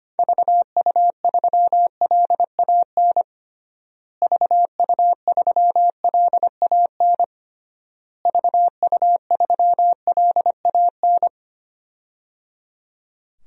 Morse code, heard as a keyed tone, about 25 wpm, 700 Hz. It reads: VU3LAN VU3LAN VU3LAN
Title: DXCC-100 Call Signs - 6 Characters - Repeated 3x - 25wpm